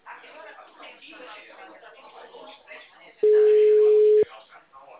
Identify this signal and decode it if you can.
Ringback tone (Europe, 425 Hz), with speech.